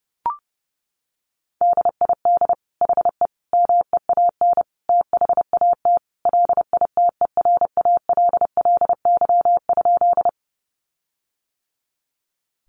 Morse code, heard as a keyed tone, about 30 wpm, 700 Hz. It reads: DID HE MEAN THAT LITERALLY?
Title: Sentences from Top 1400 Words 30wpm